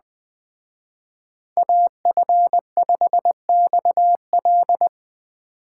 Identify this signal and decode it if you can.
Morse code, heard as a keyed tone, about 20 wpm, 700 Hz. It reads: AF5XL
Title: Speed Racing - USA 2x2 Call Signs 17wpm